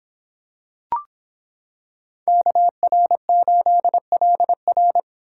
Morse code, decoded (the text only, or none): KR8LR